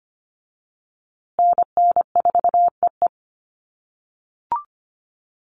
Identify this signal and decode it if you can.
Morse code, heard as a keyed tone, about 25 wpm, 700 Hz. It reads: NN4EE
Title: USA Call Signs Repeated 2x 25wpm